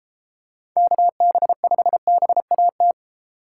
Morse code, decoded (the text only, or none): KB5BAT